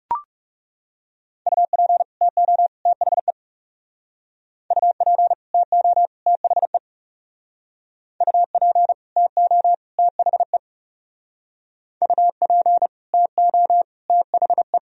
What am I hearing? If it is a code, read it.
Morse code, heard as a keyed tone, about 35 wpm, 700 Hz. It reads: UP TO THE UP TO THE UP TO THE UP TO THE
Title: Speed-Racing - Sets of 3 Words 30wpm